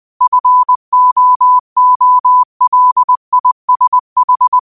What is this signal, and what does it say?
Morse code, heard as a keyed tone, about 20 wpm, 1000 Hz. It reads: FOOLISH
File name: qotdfiles / 20wpm